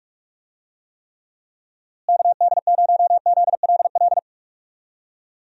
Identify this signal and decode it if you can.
Morse code, heard as a keyed tone, about 46 wpm, 700 Hz. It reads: KD0ZLL
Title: USA Call Signs - 6 Characters - Repeated 2x - 45wpm